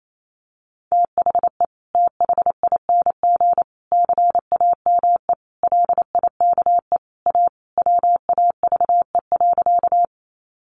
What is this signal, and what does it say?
Morse code, heard as a keyed tone, about 28 wpm, 700 Hz. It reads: THE THING CAME LIKE A WAVE.
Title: Sentences from Top 1400 Words 28wpm